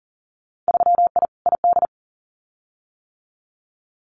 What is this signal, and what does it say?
Morse code, heard as a keyed tone, about 40 wpm, 700 Hz. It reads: 3I ID
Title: Winter Field Day Exchanges 40wpm